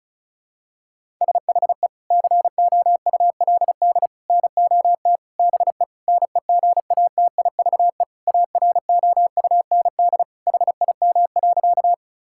Morse code, heard as a keyed tone, about 36 wpm, 700 Hz. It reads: SHE COULD NOT BE NEGATIVE AROUND HIM.
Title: Sentences from Top 1000 Words - Review Part 6 - 35wpm